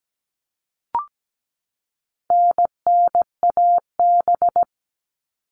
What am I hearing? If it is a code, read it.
Morse code, heard as a keyed tone, about 17 wpm, 700 Hz. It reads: NNAB
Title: All Cut Numbers - 4-Digit 17wpm